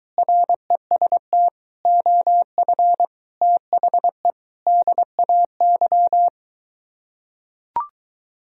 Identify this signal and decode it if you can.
Morse code, heard as a keyed tone, about 23 wpm, 700 Hz. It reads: REST OF THE DAY